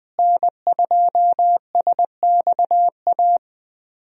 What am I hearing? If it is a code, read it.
Morse code, heard as a keyed tone, about 20 wpm, 700 Hz. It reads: N2SXA
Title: USA Call Signs - 5 Characters - Repeated 2x - 20wpm